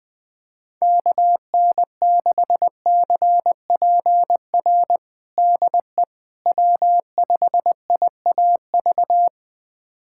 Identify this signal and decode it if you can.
Morse code, heard as a keyed tone, about 20 wpm, 700 Hz. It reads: KN6CPR DE W5IAV
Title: USA Callsign DE Callsign 20wpm